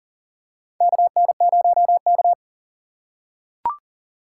Morse code, heard as a keyed tone, about 37 wpm, 700 Hz. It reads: KN0K
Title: USA Call Signs - 4 Characters - 4x Character Spacing - 40wpm